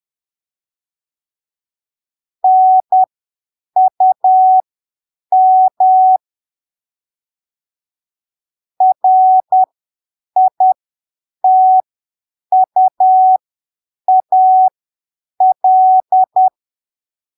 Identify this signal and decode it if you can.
Morse code, heard as a keyed tone, about 10 wpm, spaced as for 7 wpm, 750 Hz. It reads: NUM RITUAL
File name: arquivos / texto / texto 08 10ppm